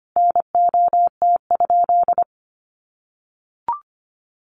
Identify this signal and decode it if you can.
Morse code, heard as a keyed tone, about 25 wpm, 700 Hz. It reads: NOT?
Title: Sentences from Top 1100 Words 25wpm